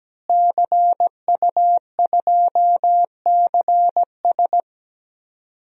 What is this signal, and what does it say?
Morse code, heard as a keyed tone, about 17 wpm, 700 Hz. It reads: CU2CS
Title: DXCC-100 Call Signs - 5 Characters - Repeated 3x - 17wpm